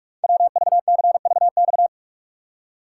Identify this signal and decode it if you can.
Morse code, heard as a keyed tone, about 44 wpm, 700 Hz. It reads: WVCVX
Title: Code Groups - 5 Letter:Numbers - 45wpm